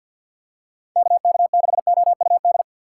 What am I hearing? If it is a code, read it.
Morse code, heard as a keyed tone, about 47 wpm, 700 Hz. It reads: KK6OUD
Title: USA Call Signs - 6 Characters - 6x Character Spacing - 50wpm